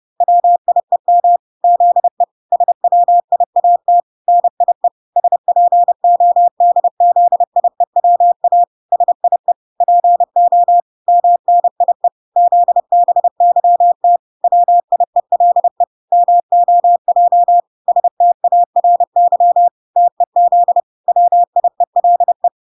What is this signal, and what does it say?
Morse code, heard as a keyed tone, about 30 wpm, 700 Hz. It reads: WIEM ZE SWIAT NIE SPODZIEWA SIE PO MNIE ZBYT WIELE MOJ STARY TEZ WIELE